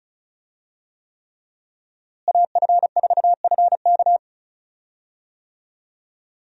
Morse code, heard as a keyed tone, about 36 wpm, 700 Hz. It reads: AF4FK
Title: USA Call Signs - 5 Characters - Repeated 3x - 35wpm